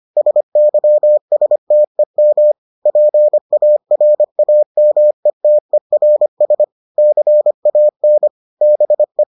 Morse code, heard as a keyed tone, about 25 wpm, 600 Hz. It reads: SYSTEM PARAMETERS CAN BE